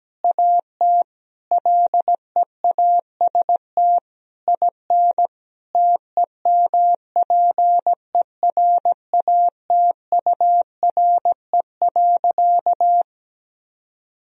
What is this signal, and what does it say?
Morse code, heard as a keyed tone, about 17 wpm, 700 Hz. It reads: AT LEAST IN TEMPERATURE.
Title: Sentences from Top 1000 Words - Review Part 4 - 17wpm